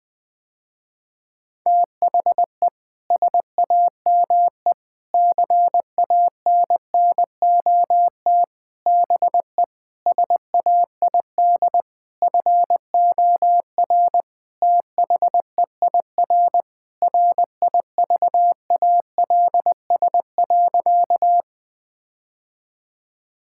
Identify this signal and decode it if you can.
Morse code, heard as a keyed tone, about 20 wpm, 700 Hz. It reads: THE SAME CANNOT BE SAID FOR THEIR RIVALS.
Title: Sentences with 7-Word Phrases - Spoken Beginning - 20wpm